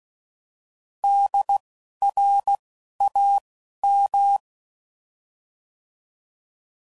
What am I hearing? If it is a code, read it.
Morse code, heard as a keyed tone, about 16 wpm, spaced as for 11 wpm, 800 Hz. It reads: DRAM
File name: Prefixes / By First Letter / Ds